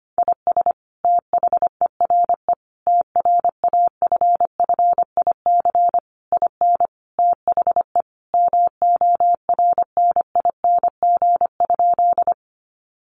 Morse code, heard as a keyed tone, about 25 wpm, 700 Hz. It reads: IS THERE TRAFFIC IN THE MORNING?
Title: Sentences from Top 1300 Words 25wpm